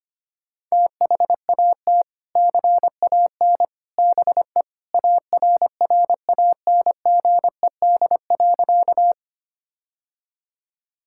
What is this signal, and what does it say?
Morse code, heard as a keyed tone, about 25 wpm, 700 Hz. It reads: THAT CAN BE ARRANGED.
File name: practice / 25wpm / Sentences with 3-Word Phrases - 25wpm